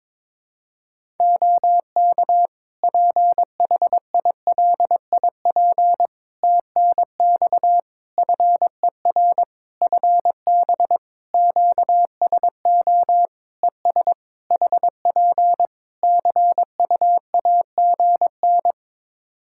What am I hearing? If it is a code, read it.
Morse code, heard as a keyed tone, about 22 wpm, 700 Hz. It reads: OK PHILIP TNX FER FB QSO ES HP CUAGN
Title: Practice QSO - Exchange 4 - Ending - 22wpm